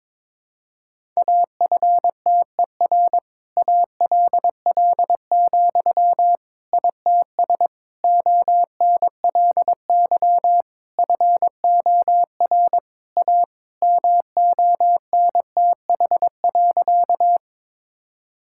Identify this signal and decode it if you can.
Morse code, heard as a keyed tone, about 22 wpm, 700 Hz. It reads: AFTER ALL, ITS ONLY FOR A MONTH.
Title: Sentences from Top 1000 Words - Review Part 5 - 22wpm